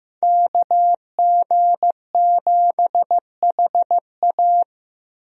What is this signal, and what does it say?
Morse code, heard as a keyed tone, about 15 wpm, 700 Hz. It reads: KG7HA
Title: USA Call Signs - 5 Characters - 15wpm